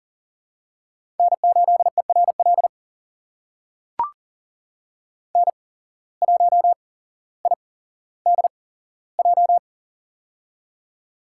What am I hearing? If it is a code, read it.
Morse code, heard as a keyed tone, about 40 wpm, spaced as for 17 wpm, 700 Hz. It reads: N8ERL N1IDJ
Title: USA Call Signs - 8x Character Spacing - 40wpm